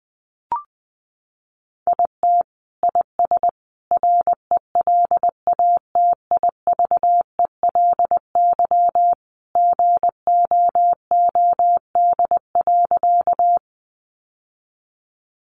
Morse code, heard as a keyed tone, about 20 wpm, 700 Hz. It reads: IT IS RELATIVELY GOOD.
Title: Sentences from Top 1100 Words 20wpm